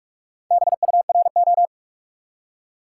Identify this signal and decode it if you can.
Morse code, heard as a keyed tone, about 53 wpm, 700 Hz. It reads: BURO
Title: Speed-Racing Top 90 QSO Elements 45wpm